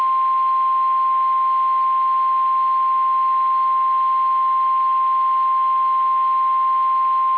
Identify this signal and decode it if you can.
Emergency Alert System (SAME) tones (the 1050 Hz attention tone). It is heard from 0.4 to 6.8 s.
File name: eas / KSDZKDJL / KSDZKDJL